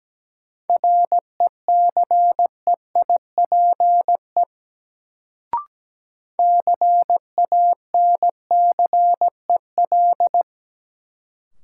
Morse code, heard as a keyed tone, about 17 wpm, 700 Hz. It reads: RECEIPE CANCEL
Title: Sentences from Top 1200 Words - New Word Drill Alt - 17wpm